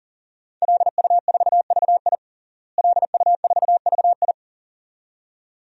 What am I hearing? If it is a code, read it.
Morse code, heard as a keyed tone, about 40 wpm, 700 Hz. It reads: LU4VI LU4VI
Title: DXCC-100 Call Signs - 5 Characters - Repeated 2x - 40wpm